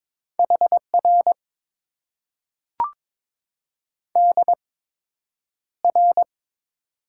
Morse code, heard as a keyed tone, about 22 wpm, spaced as for 9 wpm, 700 Hz. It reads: HR DR
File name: Top 90 QSO Elements - 8x Character Spacing - 22wpm